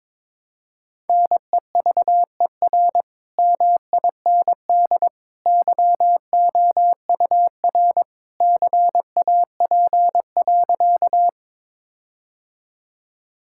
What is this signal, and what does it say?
Morse code, heard as a keyed tone, about 22 wpm, 700 Hz. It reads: NEVER MIND YOUR CAP.
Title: Sentences from Top 1600 Words 22wpm